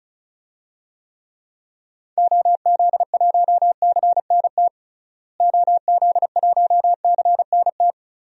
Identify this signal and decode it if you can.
Morse code, heard as a keyed tone, about 36 wpm, 700 Hz. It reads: OZ1CNT OZ1CNT
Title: DXCC-100 Call Signs - 6 Characters - Repeated 2x - 35wpm